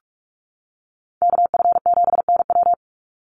Morse code, heard as a keyed tone, about 44 wpm, 700 Hz. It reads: KF7NW
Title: USA Call Signs 45wpm